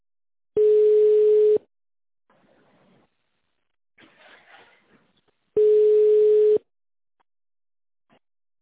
Ringback tone (Europe, 425 Hz).